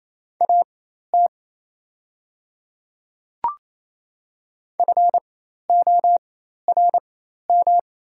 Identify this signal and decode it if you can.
Morse code, heard as a keyed tone, about 28 wpm, spaced as for 13 wpm, 690 Hz. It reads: AT FORM
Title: 4x Character Spacing - Top 200 Words - 28wpm